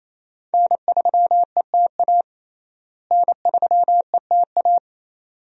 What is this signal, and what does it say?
Morse code, heard as a keyed tone, about 28 wpm, 700 Hz. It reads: N3ETA N3ETA
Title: USA Call Signs - 5 Characters - Repeated 2x - 28wpm